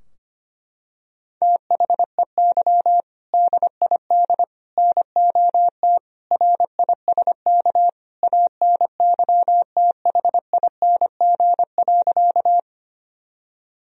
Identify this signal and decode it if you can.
Morse code, heard as a keyed tone, about 25 wpm, 700 Hz. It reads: THEY DID NOT RISK ANYTHING.
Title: Sentences from Top 1000 Words - Review Part 4 - 25wpm